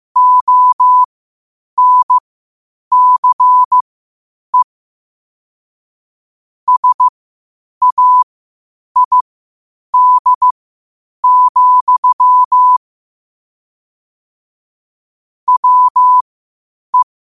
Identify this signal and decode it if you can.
Morse code, heard as a keyed tone, about 15 wpm, spaced as for 8 wpm, 1000 Hz. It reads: ONCE SAID, WE
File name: qotdfiles / 8wpm